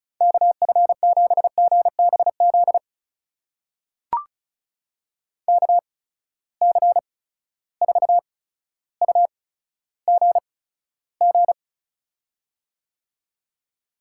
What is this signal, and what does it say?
Morse code, heard as a keyed tone, about 36 wpm, spaced as for 15 wpm, 700 Hz. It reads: KF7GBZ KC4UGG
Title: USA Call Signs - 6 Characters - 8x Character Spacing - 35wpm